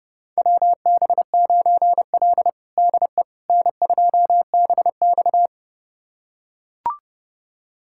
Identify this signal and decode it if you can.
Morse code, heard as a keyed tone, about 31 wpm, 700 Hz. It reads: WB9L DE N2BX
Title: USA Callsign DE Callsign - 4 Characters - 30wpm